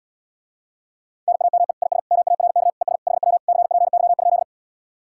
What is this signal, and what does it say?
Morse code, heard as a keyed tone, about 77 wpm, 700 Hz. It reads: NAME IS MARK ES HW CPY?